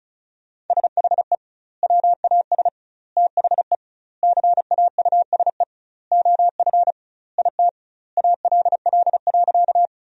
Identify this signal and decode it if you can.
Morse code, heard as a keyed tone, about 36 wpm, 700 Hz. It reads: SHE WAS THE CAUSE OF IT ALL.
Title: 2x Word-Spacing - Sentences from Top 200 Words - 35wpm